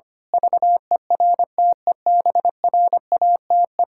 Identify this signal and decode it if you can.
Morse code, heard as a keyed tone, about 25 wpm, 700 Hz. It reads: VERTEBRATE
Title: Pangram Challenge v2 - 25wpm